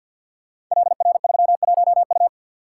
Morse code, heard as a keyed tone, about 52 wpm, 700 Hz. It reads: LR31U
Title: Code Groups - 5 Letter:Numbers - 50wpm